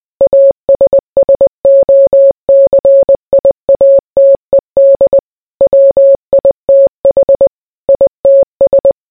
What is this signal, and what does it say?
Morse code, heard as a keyed tone, about 20 wpm, 550 Hz. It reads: ASSOCIATED WITH ITS